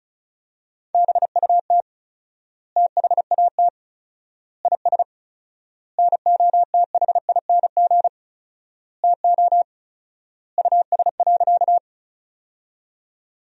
Morse code, heard as a keyed tone, about 36 wpm, 700 Hz. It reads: BUT THAT IS NOTHING TO US.
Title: 4x Word-Spacing - Sentences from Top 500 Words - 35wpm